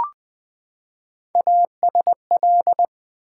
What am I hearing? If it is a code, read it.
Morse code, heard as a keyed tone, about 20 wpm, 700 Hz. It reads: ASL
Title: Build A Word - 3 Letters At A Time - 6-Letter Words - 20wpm